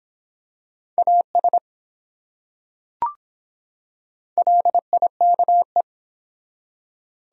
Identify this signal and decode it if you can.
Morse code, heard as a keyed tone, about 23 wpm, 700 Hz. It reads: AS LIKE